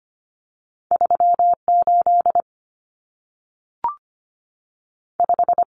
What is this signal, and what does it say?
Morse code, heard as a keyed tone, about 25 wpm, 700 Hz. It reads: 38 5
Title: Numbers - 2-Digit 25wpm